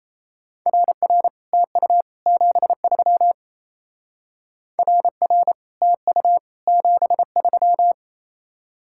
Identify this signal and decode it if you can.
Morse code, heard as a keyed tone, about 36 wpm, 700 Hz. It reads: RR TU 73 RR TU 73